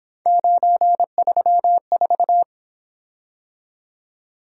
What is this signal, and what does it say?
Morse code, heard as a keyed tone, about 27 wpm, 700 Hz. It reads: 934
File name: Speed-Racing - Three Letter-Number Combinations - 22wpm